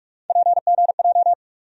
Morse code, heard as a keyed tone, about 48 wpm, 700 Hz. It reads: PGJ
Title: USA Call Sign Postfixes - 3 Letters 45wpm